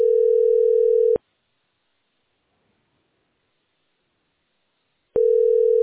Ringback tone (North America), 440 + 480 Hz.